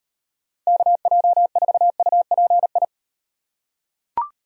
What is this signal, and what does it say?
Morse code, heard as a keyed tone, about 38 wpm, 700 Hz. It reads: KJ4UPI